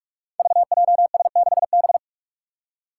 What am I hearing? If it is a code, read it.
Morse code, heard as a keyed tone, about 46 wpm, 700 Hz. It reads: VJS6B
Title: Five Challenging Letter-Number Combinations 45wpm